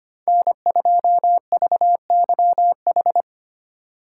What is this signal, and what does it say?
Morse code, heard as a keyed tone, about 31 wpm, 700 Hz. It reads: N2VYH
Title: Speed Racing - USA 1x3 Call Signs 25wpm